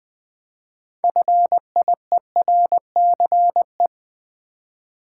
Morse code, heard as a keyed tone, about 20 wpm, 700 Hz. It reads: FIERCE
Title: Build A Word - 3 Letters At A Time - 6-Letter Words - 20wpm